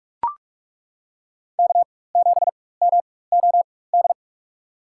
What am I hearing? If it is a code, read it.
Morse code, heard as a keyed tone, about 42 wpm, spaced as for 21 wpm, 700 Hz. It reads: K7MOD
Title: USA Call Signs - 5 Characters - 4x Character Spacing - 45wpm